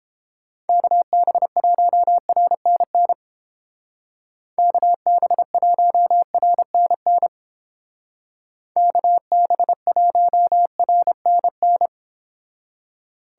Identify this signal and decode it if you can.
Morse code, heard as a keyed tone, about 29 wpm, 700 Hz. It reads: KB1RNN KB1RNN KB1RNN